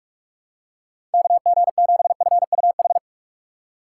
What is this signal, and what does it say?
Morse code, heard as a keyed tone, about 44 wpm, 700 Hz. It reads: KG7FUH
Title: USA Call Signs - 6 Characters - 4x Character Spacing - 45wpm